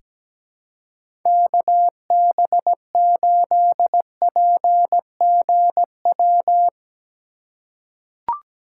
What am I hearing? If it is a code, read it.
Morse code, heard as a keyed tone, about 17 wpm, 700 Hz. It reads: KB8PGW
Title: K1USN SST QSO Practice - 17wpm